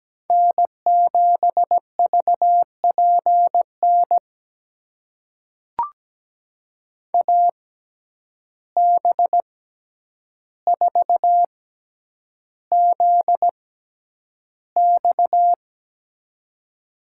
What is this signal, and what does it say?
Morse code, heard as a keyed tone, about 17 wpm, spaced as for 9 wpm, 700 Hz. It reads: N7VPN AB4ZX
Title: USA Call Signs - 5 Characters - 6x Character Spacing - 17wpm